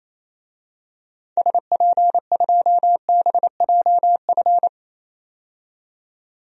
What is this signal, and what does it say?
Morse code, heard as a keyed tone, about 27 wpm, 700 Hz. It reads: SP2BJF
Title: DXCC-100 Call Signs - 6 Characters - 28wpm